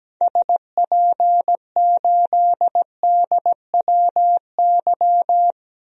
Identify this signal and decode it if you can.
Morse code, heard as a keyed tone, about 17 wpm, 700 Hz. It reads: SP8DWY
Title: DXCC-100 Call Signs - 6 Characters - 17wpm